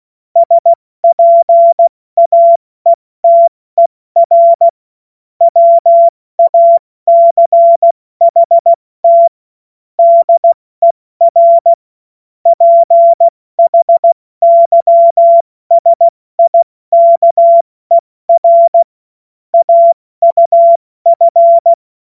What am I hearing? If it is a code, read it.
Morse code, heard as a keyed tone, about 16 wpm, 680 Hz. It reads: SPAETER WACHT DER PHYSIKER AUF
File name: Morsekurs / Witze / Witz 14. Juni 2022 70 F80